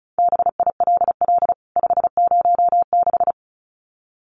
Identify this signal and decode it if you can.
Morse code, heard as a keyed tone, about 34 wpm, 700 Hz. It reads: BILL 506
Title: CWT Exchange 35wpm